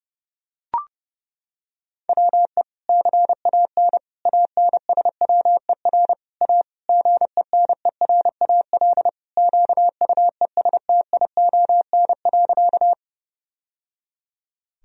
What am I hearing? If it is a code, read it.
Morse code, heard as a keyed tone, about 30 wpm, 700 Hz. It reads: WE CAN ANSWER A GENERAL QUESTION.
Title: Sentences from Top 1000 Words - Review Part 3 - 30wpm